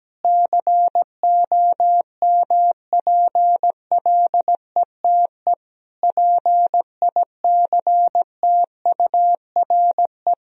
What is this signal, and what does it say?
Morse code, heard as a keyed tone, about 17 wpm, 700 Hz. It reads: COMPLETE PICTURE
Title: Sentences from Top 1000 Words - Review Part 2 - 17wpm